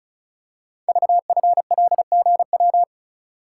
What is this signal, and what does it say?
Morse code, heard as a keyed tone, about 36 wpm, 700 Hz. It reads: VFLGW
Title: Code Groups - 5 Letters - 35wpm